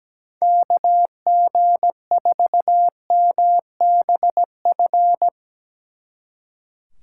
Morse code, heard as a keyed tone, about 18 wpm, 700 Hz. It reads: KG4MBF
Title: Speed Racing - USA 2x3 Call Signs 17wpm